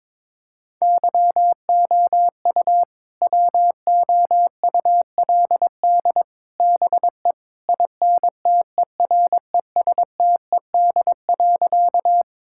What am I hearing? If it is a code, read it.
Morse code, heard as a keyed tone, about 21 wpm, 700 Hz. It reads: YOU WOULD BE INTERESTED.
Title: Sentences from Top 1000 Words - Review Part 6 - 22wpm